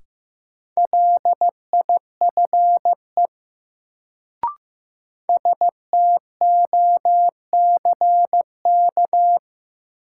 Morse code, heard as a keyed tone, about 15 wpm, 700 Hz. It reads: LIFE STOCK
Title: Sets of 5 Words - New Word Drill - 15wpm